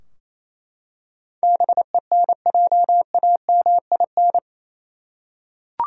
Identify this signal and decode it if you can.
Morse code, heard as a keyed tone, about 28 wpm, 700 Hz. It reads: BENJAMIN
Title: USA Common Names - 28wpm